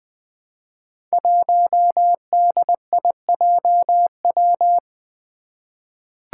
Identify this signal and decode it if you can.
Morse code, heard as a keyed tone, about 20 wpm, 700 Hz. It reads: 1DIJW
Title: Five Challenging Letter-Number Combinations 20wpm